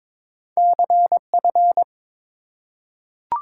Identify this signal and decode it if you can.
Morse code, heard as a keyed tone, about 22 wpm, 700 Hz. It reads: CF
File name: practice / 22wpm / Two Letter-Number Combinations - 22wpm